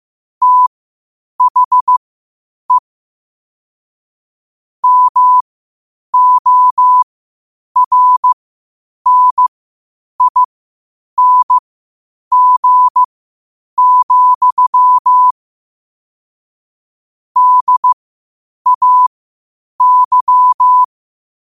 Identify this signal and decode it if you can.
Morse code, heard as a keyed tone, about 15 wpm, spaced as for 8 wpm, 1000 Hz. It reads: THE MORNING, DAY
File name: qotdfiles / 8wpm